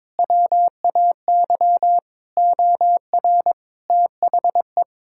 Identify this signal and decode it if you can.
Morse code, heard as a keyed tone, about 21 wpm, 700 Hz. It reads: WAY OR THE
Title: Sentences from Top 1000 Words - Review Part 6 - 22wpm